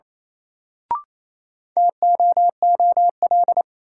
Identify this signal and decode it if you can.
Morse code, heard as a keyed tone, about 28 wpm, 700 Hz. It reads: TOOL
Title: Sentences from Top 1000 Words - Review Part 6 - New Word Drill - 28wpm